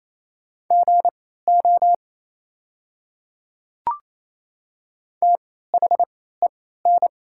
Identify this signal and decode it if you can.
Morse code, heard as a keyed tone, about 27 wpm, spaced as for 16 wpm, 690 Hz. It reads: GO THEN
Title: 3x Character Spacing - Top 100 Words - 28wpm